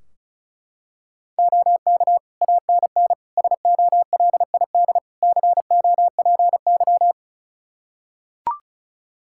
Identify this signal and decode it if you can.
Morse code, heard as a keyed tone, about 34 wpm, 700 Hz. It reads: OK ANN SOLID COPY
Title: K1USN SST QSO Practice 35wpm